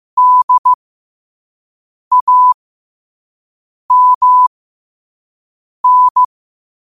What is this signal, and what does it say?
Morse code, heard as a keyed tone, about 15 wpm, spaced as for 5 wpm, 1000 Hz. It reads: DAMN
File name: qotdfiles / 5wpm